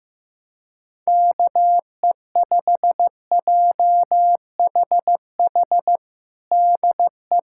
Morse code, heard as a keyed tone, about 14 wpm, 700 Hz. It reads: KE5JHH DE
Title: USA Callsign DE Callsign - 6 Characters - 15wpm